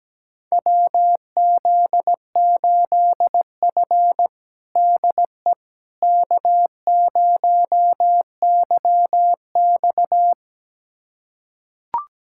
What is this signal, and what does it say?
Morse code, heard as a keyed tone, about 17 wpm, 700 Hz. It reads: WZ8F DE K0YX
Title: USA Callsign DE Callsign - 4 Characters - 17wpm